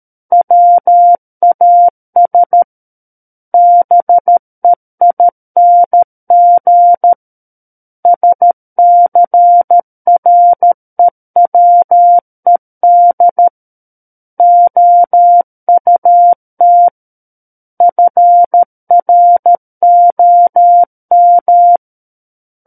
Morse code, heard as a keyed tone, about 13 wpm, 700 Hz. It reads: WAS BEING SCREWED OUT FROM
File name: Book One / 04-Chapter 4 - 13 WPM